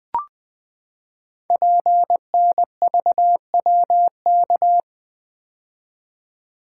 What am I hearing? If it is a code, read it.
Morse code, heard as a keyed tone, about 20 wpm, 700 Hz. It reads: PNVWK
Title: Code Groups - 5 Letters - 20wpm